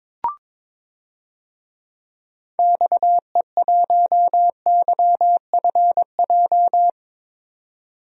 Morse code, heard as a keyed tone, about 22 wpm, 700 Hz. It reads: XE1YFJ
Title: DXCC-100 Call Signs - 6 Characters - 22wpm